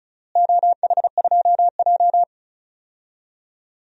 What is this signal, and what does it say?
Morse code, heard as a keyed tone, about 36 wpm, 700 Hz. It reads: OH2J
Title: DXCC-100 Call Signs - 4 Characters - Repeated 2x - 35wpm